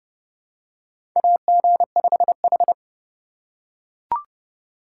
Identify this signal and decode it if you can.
Morse code, heard as a keyed tone, about 30 wpm, 700 Hz. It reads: AG5H
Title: USA Call Signs - 4 Characters - 2x Character Spacing - 30wpm